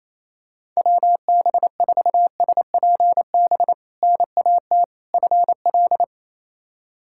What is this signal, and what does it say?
Morse code, heard as a keyed tone, about 28 wpm, 700 Hz. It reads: WB4SPB NAT FL
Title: K1USN SST QSO Practice - 28wpm